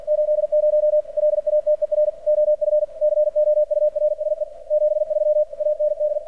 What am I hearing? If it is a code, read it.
Morse code, heard as a keyed tone, about 47 wpm, 600 Hz, with radio noise. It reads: 80 LATER OF COURSE 73 AND
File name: archived / CRSnet